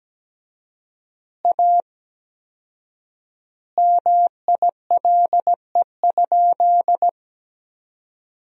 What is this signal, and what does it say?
Morse code, heard as a keyed tone, about 17 wpm, 700 Hz. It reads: A MILE?
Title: 4x Word-Spacing - Sentences from Top 400 Words - 17wpm